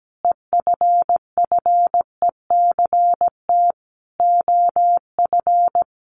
Morse code, heard as a keyed tone, about 17 wpm, 700 Hz. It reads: EFFECT OF
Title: Sentences from Top 1400 Words 17wpm